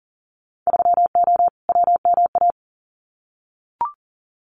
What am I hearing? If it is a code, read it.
Morse code, heard as a keyed tone, about 38 wpm, 700 Hz. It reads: 3O WMA
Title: Winter Field Day Exchanges Encore 40wpm